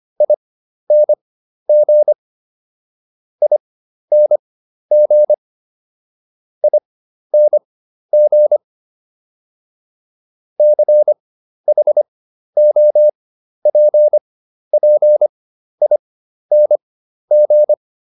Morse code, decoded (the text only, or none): ING ING ING CHOPPING